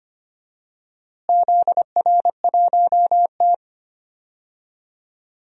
Morse code, heard as a keyed tone, about 25 wpm, 700 Hz. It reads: ZR1T
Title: DXCC-100 Call Signs - 4 Characters - Repeated 3x - 25wpm